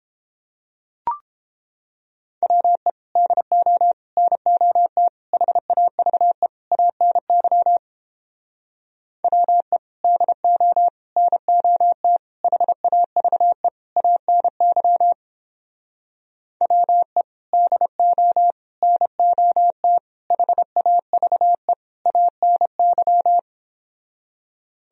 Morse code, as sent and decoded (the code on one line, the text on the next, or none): .-- . / -.. --- / -. --- - / .... .- ...- . / .- -. -.-- / .-- . / -.. --- / -. --- - / .... .- ...- . / .- -. -.-- / .-- . / -.. --- / -. --- - / .... .- ...- . / .- -. -.--
WE DO NOT HAVE ANY WE DO NOT HAVE ANY WE DO NOT HAVE ANY